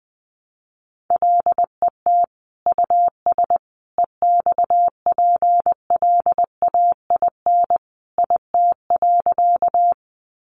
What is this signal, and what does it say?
Morse code, heard as a keyed tone, about 20 wpm, 700 Hz. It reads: LET US EXPLAIN IT.
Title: Sentences from Top 1100 Words 20wpm